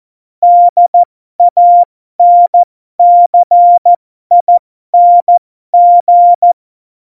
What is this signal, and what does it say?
Morse code, heard as a keyed tone, about 14 wpm, spaced as for 12 wpm, 700 Hz. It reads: DANCING
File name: Morsekurs / Witze